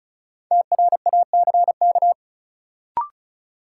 Morse code, heard as a keyed tone, about 34 wpm, 700 Hz. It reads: TRACK
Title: Sentences from Top 1100 Words - New Word Drill - 35wpm